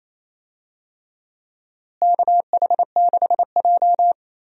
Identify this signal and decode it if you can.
Morse code, heard as a keyed tone, about 28 wpm, 700 Hz. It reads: KH6J
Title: USA Call Signs - 4 Characters - Repeated 3x - 28wpm